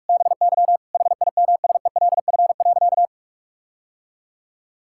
Morse code, heard as a keyed tone, about 41 wpm, 700 Hz. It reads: BY HIMSELF.
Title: Sentences from Top 1600 Words 45wpm